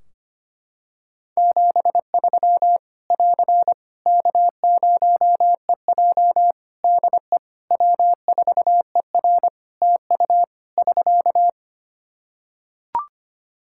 Morse code, heard as a keyed tone, about 25 wpm, 700 Hz. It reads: 73 <AR> K0EJ DE W4ER TU <SK>
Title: Practice QSO - Exchange 4 - Ending - 25wpm